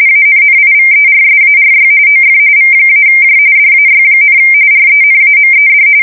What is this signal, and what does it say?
RTTY, about 45.45 baud. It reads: ANNIVERSARY OF V-J DAY, RENAMED
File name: fsk170 / battleshipnj